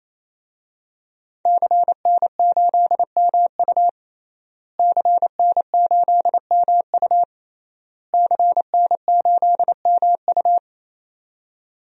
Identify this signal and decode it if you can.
Morse code, heard as a keyed tone, about 28 wpm, 700 Hz. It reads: CN8MU CN8MU CN8MU
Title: DXCC-100 Call Signs - 5 Characters - Repeated 3x - 28wpm